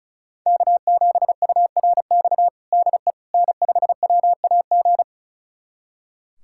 Morse code, heard as a keyed tone, about 36 wpm, 700 Hz. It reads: K7URX DE N5WAG
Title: USA Callsign DE Callsign - 5 Characters - 35wpm